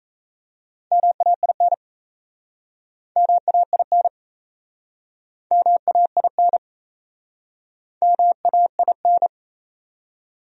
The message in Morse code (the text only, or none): MAIN MAIN MAIN MAIN